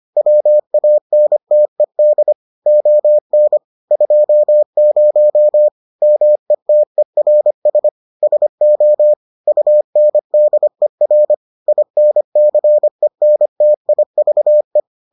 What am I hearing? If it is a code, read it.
Morse code, heard as a keyed tone, about 25 wpm, 600 Hz. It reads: WANTED ON 20 METERS SO UNDER INCENTIVE